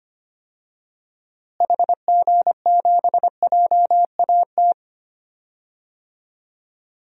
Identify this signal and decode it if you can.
Morse code, heard as a keyed tone, about 25 wpm, 700 Hz. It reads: HG7JAT
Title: DXCC-100 Call Signs - 6 Characters - Repeated 2x - 25wpm